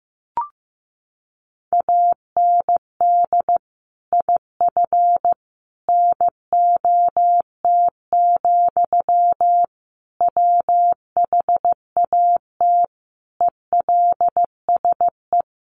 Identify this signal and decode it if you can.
Morse code, heard as a keyed tone, about 15 wpm, 700 Hz. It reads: AND IF NOT, WHAT ELSE
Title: Sentences from Top 1400 Words 15wpm